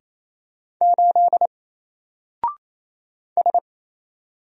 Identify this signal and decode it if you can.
Morse code, heard as a keyed tone, about 28 wpm, 700 Hz. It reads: 8 S
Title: Single Letter-Number 28wpm